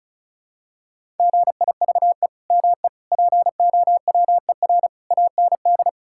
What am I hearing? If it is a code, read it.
Morse code, heard as a keyed tone, about 36 wpm, 700 Hz. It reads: GIVE ME POWER AND
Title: Sentences from Top 1000 Words - Review Part 5 - 35wpm